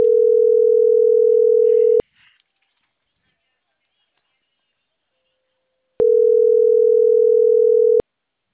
Ringback tone (North America), 440 + 480 Hz.
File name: CENTRAL CAEA / out agcaea03